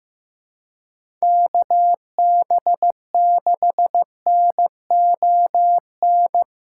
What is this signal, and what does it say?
Morse code, heard as a keyed tone, about 16 wpm, 700 Hz. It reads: KB6NON
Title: USA Callsign DE Callsign 15wpm